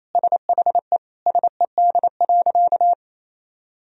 Morse code, heard as a keyed tone, about 28 wpm, 700 Hz. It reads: SHE SED.
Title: Sentences from Top 100 Words and Abbreviations 28wpm